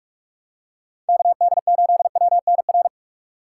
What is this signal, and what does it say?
Morse code, heard as a keyed tone, about 43 wpm, 690 Hz. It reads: KD8WNR
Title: USA Call Signs - 6 Characters - 2x Character Spacing - 45wpm